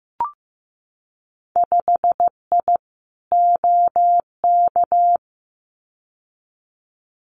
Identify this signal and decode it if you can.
Morse code, heard as a keyed tone, about 15 wpm, 700 Hz. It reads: 5I OK
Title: Winter Field Day Exchanges Encore 15wpm